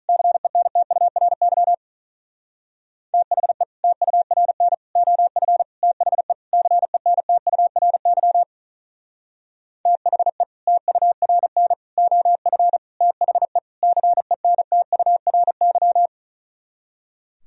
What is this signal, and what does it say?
Morse code, heard as a keyed tone, about 42 wpm, 700 Hz. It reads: CENTURY THE TURN OF THE CENTURY THE TURN OF THE CENTURY
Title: Speed-Racing - Sets of 5 Words 35wpm